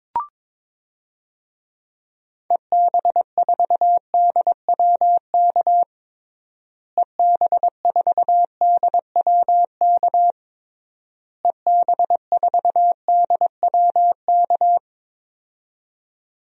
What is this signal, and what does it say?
Morse code, heard as a keyed tone, about 22 wpm, 700 Hz. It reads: EB4DWK EB4DWK EB4DWK
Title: DXCC-100 Call Signs - 6 Characters - Repeated 3x - 22wpm